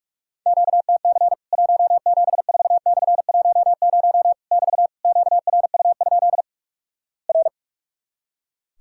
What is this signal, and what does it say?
Morse code, heard as a keyed tone, about 46 wpm, 700 Hz. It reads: QTC 174/10 <BT> QRU? R
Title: Worked All Europe DX Contest - QTC Exchange - 45wpm